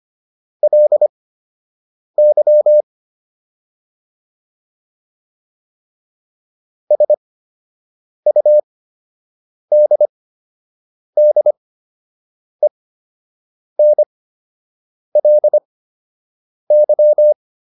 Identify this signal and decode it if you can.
Morse code, heard as a keyed tone, about 25 wpm, spaced as for 7 wpm, 600 Hz. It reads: LY SUDDENLY